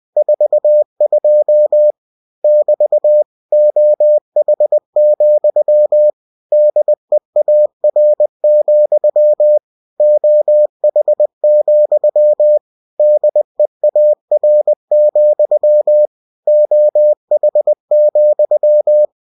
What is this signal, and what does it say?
Morse code, heard as a keyed tone, about 20 wpm, 600 Hz. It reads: 42 <BT> OH, DEAR, OH, DEAR, OH,
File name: cw-books / winnie-the-pooh / winnie-20wpm-page-0042